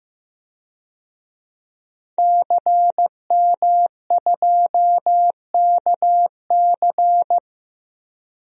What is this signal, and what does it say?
Morse code, heard as a keyed tone, about 15 wpm, 700 Hz. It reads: CM2KC
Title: DXCC-100 Call Signs - 5 Characters - 15wpm